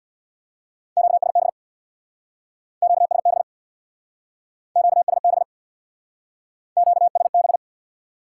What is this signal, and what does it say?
Morse code, heard as a keyed tone, about 66 wpm, 700 Hz. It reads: QSB QSB QSB QSB